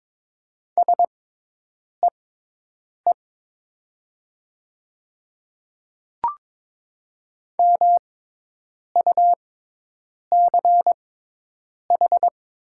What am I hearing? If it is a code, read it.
Morse code, heard as a keyed tone, about 22 wpm, spaced as for 8 wpm, 700 Hz. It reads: SEE MUCH
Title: 6x Character Spacing - Top 200 Words - 22wpm